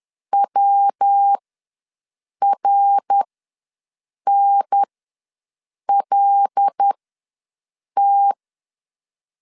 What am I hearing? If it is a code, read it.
Morse code, heard as a keyed tone, about 10 wpm, spaced as for 6 wpm, 800 Hz. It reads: WRNLT
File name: LEZIONI / 5 LBVC / 5 L ws 10wpm 3636